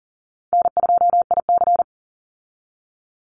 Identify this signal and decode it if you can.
Morse code, heard as a keyed tone, about 40 wpm, 700 Hz. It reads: N2IC
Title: Call Signs Repeated 2x - 40wpm